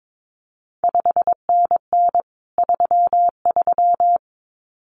Morse code, heard as a keyed tone, about 22 wpm, 700 Hz. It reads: 5NN 33
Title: CQ WW DX Contest Exchange Encore - 22wpm